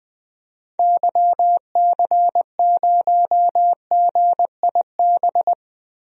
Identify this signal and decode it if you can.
Morse code, heard as a keyed tone, about 21 wpm, 700 Hz. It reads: YC0GIB
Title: DXCC-100 Call Signs - 6 Characters - 20wpm